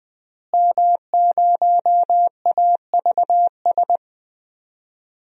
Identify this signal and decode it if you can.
Morse code, heard as a keyed tone, about 21 wpm, 700 Hz. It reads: M0AVS
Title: DXCC-100 Call Signs - 5 Characters - Repeated 3x - 20wpm